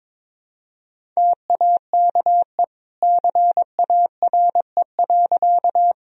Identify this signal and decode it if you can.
Morse code, heard as a keyed tone, about 22 wpm, 700 Hz. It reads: TAKE CARE.
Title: Sentences from Top 1000 Words - Review Part 1 - 22wpm